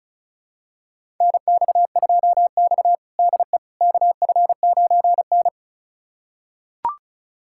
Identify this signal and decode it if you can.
Morse code, heard as a keyed tone, about 36 wpm, 700 Hz. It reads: NX2X DE KF9N
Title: USA Callsign DE Callsign - 4 Characters - 35wpm